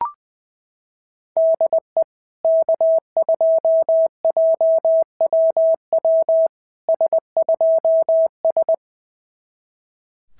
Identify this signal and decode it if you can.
Morse code, heard as a keyed tone, about 20 wpm, 690 Hz. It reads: DE K2JWW S2S
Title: SOTA QSO Practice 20wpm